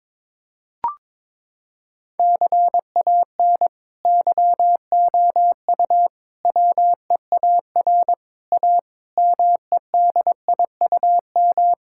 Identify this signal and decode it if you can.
Morse code, heard as a keyed tone, about 22 wpm, 700 Hz. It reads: CAN YOU WEAR A MEDIUM
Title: Sentences from Top 1000 Words - Review Part 6 - 22wpm